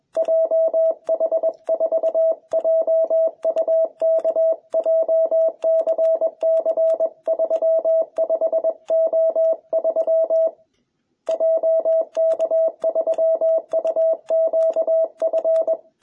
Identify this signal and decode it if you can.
Morse code, heard as a keyed tone, about 21 wpm, 660 Hz. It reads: JH4JUK JCC35O3 JK3UQF